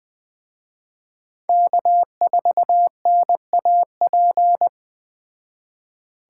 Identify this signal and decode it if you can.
Morse code, heard as a keyed tone, about 20 wpm, 700 Hz. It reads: K4NAP
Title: Speed Racing - USA 1x3 Call Signs 17wpm